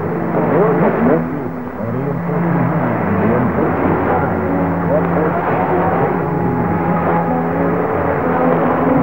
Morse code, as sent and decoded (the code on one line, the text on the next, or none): none